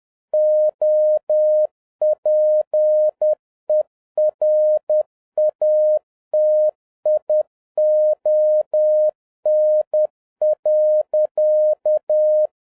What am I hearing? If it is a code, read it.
Morse code, heard as a keyed tone, about 10 wpm, 610 Hz. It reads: OPERATION.
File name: 2011-07-mp3 / n41 EXCITING QRP CW OPERATION 10wpm